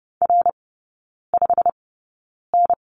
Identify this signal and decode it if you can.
Morse code, heard as a keyed tone, about 30 wpm, spaced as for 9 wpm, 700 Hz. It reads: R5N
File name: course / 30:10wpm / 087 Lesson 15e - Number 5 - Three Characters 30wpm